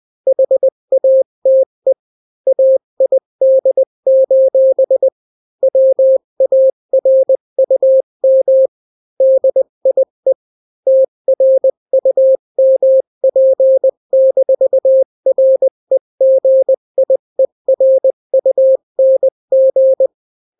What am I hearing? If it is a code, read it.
Morse code, heard as a keyed tone, about 20 wpm, 530 Hz. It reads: HATE AID: WARUM DIE TRUMP-REGIERUNG
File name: daily / 18wpm